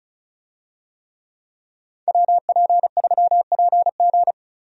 Morse code, heard as a keyed tone, about 36 wpm, 700 Hz. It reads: WP3PG
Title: USA Call Signs - 5 Characters - Repeated 3x - 35wpm